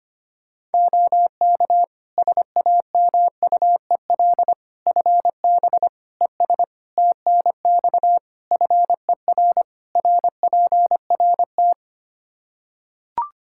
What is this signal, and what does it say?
Morse code, heard as a keyed tone, about 25 wpm, 700 Hz. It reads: OK SAMUEL FB ES TNX FER RPRT
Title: K1USN SST QSO Practice 25wpm